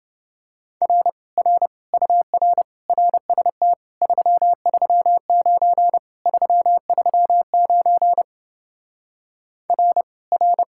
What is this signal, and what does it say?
Morse code, heard as a keyed tone, about 28 wpm, 700 Hz. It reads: R R UR RST 339 339 R R